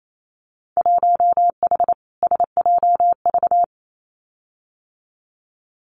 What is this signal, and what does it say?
Morse code, heard as a keyed tone, about 28 wpm, 700 Hz. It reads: 1H SJV
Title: Winter Field Day Exchanges 28wpm